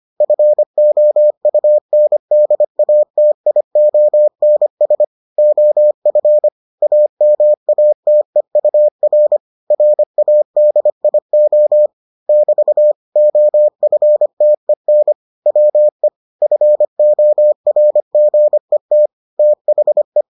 Morse code, heard as a keyed tone, about 25 wpm, 600 Hz. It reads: FOUNDATIONS OF AMATEUR RADIO <BT> OFTEN WE FORGET THE